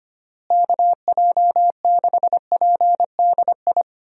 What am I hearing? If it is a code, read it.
Morse code, heard as a keyed tone, about 25 wpm, 700 Hz. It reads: KJ6PDI
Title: USA Call Signs - 6 Characters - Repeated 3x - 25wpm